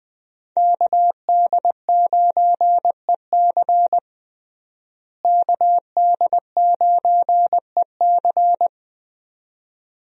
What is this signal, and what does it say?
Morse code, heard as a keyed tone, about 21 wpm, 700 Hz. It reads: KD9EC KD9EC
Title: USA Call Signs - 5 Characters - Repeated 2x - 20wpm